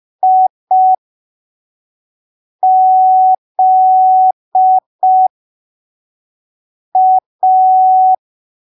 Morse code, heard as a keyed tone, about 5 wpm, spaced as for 3 wpm, 750 Hz. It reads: IZA
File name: arquivos / texto / texto 06 05ppm